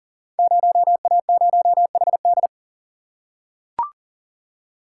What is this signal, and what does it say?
Morse code, heard as a keyed tone, about 40 wpm, 700 Hz. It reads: 0A0HD
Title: Code Groups - 5 Letter:Numbers - 40wpm